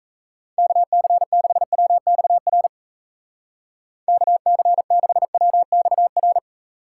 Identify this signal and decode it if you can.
Morse code, heard as a keyed tone, about 35 wpm, 700 Hz. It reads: KC6WXR KC6WXR